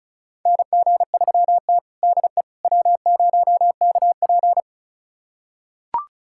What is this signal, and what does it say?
Morse code, heard as a keyed tone, about 34 wpm, 700 Hz. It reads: NG3T DE W0KP
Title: USA Callsign DE Callsign - 4 Characters - 35wpm